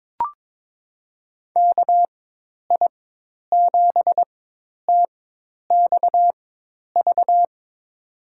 Morse code, heard as a keyed tone, about 22 wpm, spaced as for 10 wpm, 700 Hz. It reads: KI7TXV
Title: USA Call Signs - 6 Characters - 4x Character Spacing - 22wpm